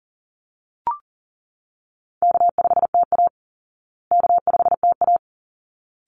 Morse code, heard as a keyed tone, about 37 wpm, 700 Hz. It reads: K5TA K5TA
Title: Easy Call Signs Repeated 2x - 40wpm